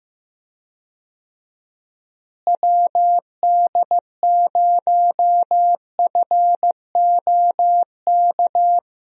Morse code, heard as a keyed tone, about 16 wpm, 700 Hz. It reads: WD0FOK
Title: USA Call Signs - 6 Characters - Repeated 3x - 15wpm